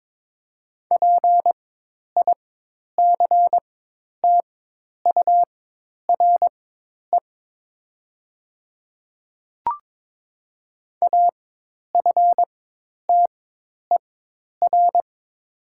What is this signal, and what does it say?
Morse code, heard as a keyed tone, about 21 wpm, spaced as for 10 wpm, 700 Hz. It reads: PICTURE AFTER
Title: 4x Character Spacing - Top 200 Words - 22wpm